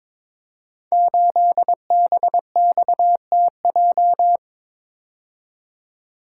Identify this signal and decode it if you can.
Morse code, heard as a keyed tone, about 22 wpm, 700 Hz. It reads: 8BXTJ
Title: Code Groups - 5 Letter:Numbers - 22wpm